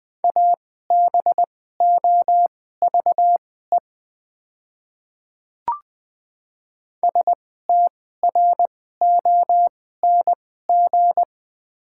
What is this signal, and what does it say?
Morse code, heard as a keyed tone, about 20 wpm, spaced as for 14 wpm, 690 Hz. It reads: ABOVE STRONG
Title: 2x Character Spacing - Top 500 Words - 20wpm